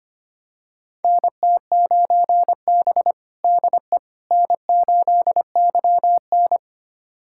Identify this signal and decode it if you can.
Morse code, heard as a keyed tone, about 25 wpm, 700 Hz. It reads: NT9B DE N8YN
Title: USA Callsign DE Callsign - 4 Characters - 25wpm